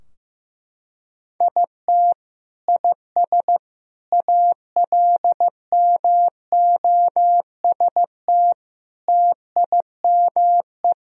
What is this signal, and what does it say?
Morse code, heard as a keyed tone, about 15 wpm, 700 Hz. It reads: IT IS ALMOST TIME
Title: Sentences from Top 1000 Words - Review Part 4 - 15wpm